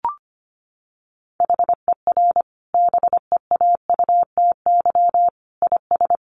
Morse code, heard as a keyed tone, about 25 wpm, 700 Hz. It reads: HER BEAUTY IS
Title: Sentences from Top 1100 Words 25wpm